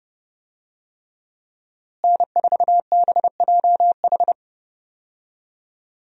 Morse code, heard as a keyed tone, about 30 wpm, 700 Hz. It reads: N4BJH
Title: USA Call Signs - 5 Characters - Repeated 3x - 30wpm